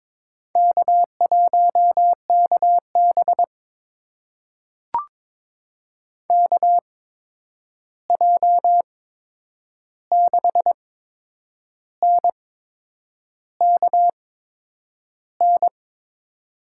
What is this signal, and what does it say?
Morse code, heard as a keyed tone, about 22 wpm, spaced as for 8 wpm, 700 Hz. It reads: K1KB KJ6NKN
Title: USA Call Signs - 8x Character Spacing - 22wpm